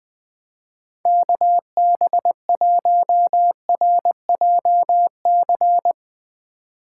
Morse code, heard as a keyed tone, about 20 wpm, 700 Hz. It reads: KB1RJC
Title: 2023 SOTA Chaser Call Signs - 20wpm